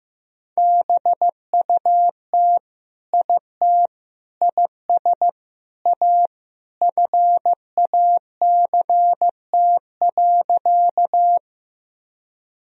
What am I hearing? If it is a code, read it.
Morse code, heard as a keyed tone, about 15 wpm, 700 Hz. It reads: BUT IT IS A FACT.
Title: Sentences from Top 1000 Words - Review Part 3 - 15wpm